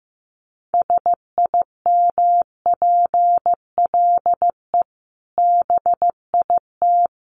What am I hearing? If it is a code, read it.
Morse code, heard as a keyed tone, about 15 wpm, 700 Hz. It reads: SIMPLE BIT
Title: Sentences from Top 900 Words 15wpm